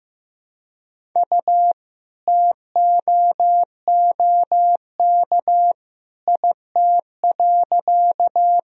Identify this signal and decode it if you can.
Morse code, heard as a keyed tone, about 15 wpm, 700 Hz. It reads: U TOOK IT.
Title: Sentences from Top 300 Words and Abbreviations 15wpm